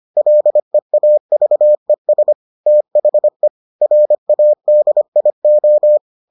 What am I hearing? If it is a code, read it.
Morse code, heard as a keyed tone, about 25 wpm, 600 Hz. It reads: LEAVES THE RADIO